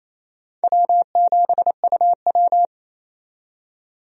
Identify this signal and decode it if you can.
Morse code, heard as a keyed tone, about 29 wpm, 690 Hz. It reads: W7UW
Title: USA Call Signs - 4 Characters - 28wpm